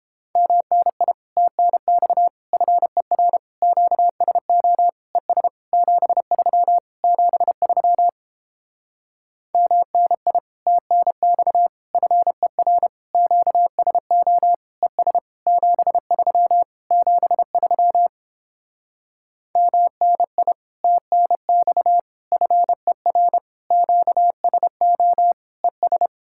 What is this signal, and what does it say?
Morse code, heard as a keyed tone, about 29 wpm, 700 Hz. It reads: MNI TNX FER QSO ES 73 73 MNI TNX FER QSO ES 73 73 MNI TNX FER QSO ES